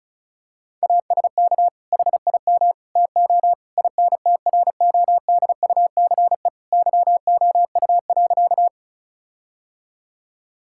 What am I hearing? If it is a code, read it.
Morse code, heard as a keyed tone, about 34 wpm, 700 Hz. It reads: ASK HIM TO INTRODUCE YOU.
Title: Sentences from Top 1600 Words 35wpm